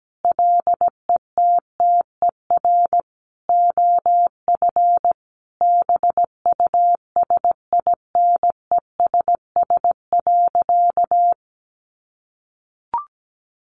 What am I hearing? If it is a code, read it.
Morse code, heard as a keyed tone, about 17 wpm, 700 Hz. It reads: LETTER OF BUSINESS.
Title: Sentences from Top 1400 Words 17wpm